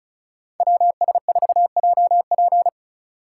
Morse code, heard as a keyed tone, about 34 wpm, 700 Hz. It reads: WS4JP
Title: USA Call Signs - 5 Characters - 4x Character Spacing - 35wpm